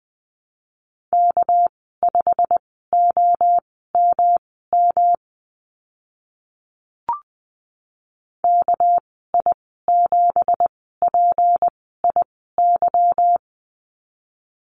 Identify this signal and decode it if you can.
Morse code, heard as a keyed tone, about 20 wpm, spaced as for 14 wpm, 700 Hz. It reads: K5OMM KI7PIY
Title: 213 Lesson 34g - Number 8 - Call Signs 20wpm